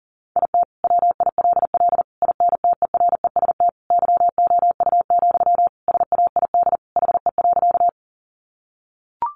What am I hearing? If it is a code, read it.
Morse code, heard as a keyed tone, about 40 wpm, 700 Hz. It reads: IT WILL INTEREST YOU, SAID HE.
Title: Sentences from Top 400 Words 40wpm